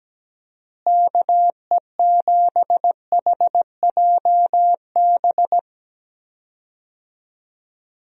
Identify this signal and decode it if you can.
Morse code, heard as a keyed tone, about 17 wpm, 700 Hz. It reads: KE7HJB
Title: USA Call Signs - 6 Characters - Repeated 2x - 17wpm